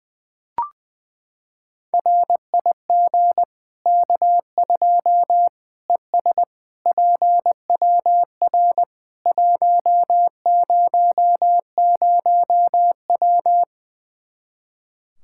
Morse code, heard as a keyed tone, about 20 wpm, 700 Hz. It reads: RIG K2 ES PWR 100W
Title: Practice QSO - Exchange 2 - Info - 20wpm